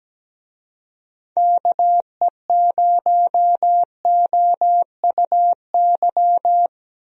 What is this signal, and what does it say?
Morse code, heard as a keyed tone, about 17 wpm, 700 Hz. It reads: KE0OUY
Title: USA Call Signs - 6 Characters - Repeated 3x - 17wpm